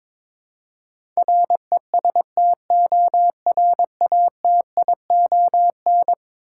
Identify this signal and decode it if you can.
Morse code, heard as a keyed tone, about 21 wpm, 700 Hz. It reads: RESTORATION
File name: practice / 22wpm / Eleven Letter Words - 22wpm